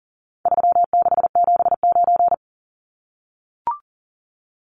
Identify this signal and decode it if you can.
Morse code, heard as a keyed tone, about 40 wpm, 700 Hz. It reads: 3679
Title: Numbers - 4-Digit 40wpm